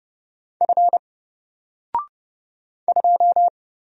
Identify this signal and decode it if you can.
Morse code, heard as a keyed tone, about 29 wpm, 700 Hz. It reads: F 2
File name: Single-Letter-Number-30wpm